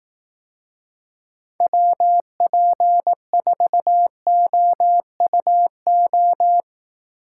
Morse code, heard as a keyed tone, about 18 wpm, 700 Hz. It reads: WP4OUO
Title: Speed Racing - USA 2x3 Call Signs 15wpm